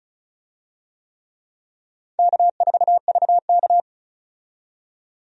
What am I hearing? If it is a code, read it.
Morse code, heard as a keyed tone, about 36 wpm, 700 Hz. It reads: K4VK
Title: USA Call Signs - 4 Characters - 35wpm